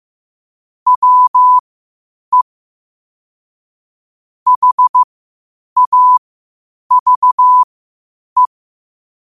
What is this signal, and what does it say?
Morse code, heard as a keyed tone, about 15 wpm, spaced as for 8 wpm, 1000 Hz. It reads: WE HAVE
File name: qotdfiles / 8wpm